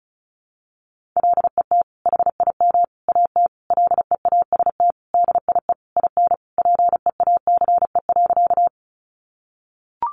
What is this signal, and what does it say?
Morse code, heard as a keyed tone, about 34 wpm, 700 Hz. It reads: LET HIM AT LEAST DIE IN PEACE.
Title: Sentences from Top 1500 Words 35wpm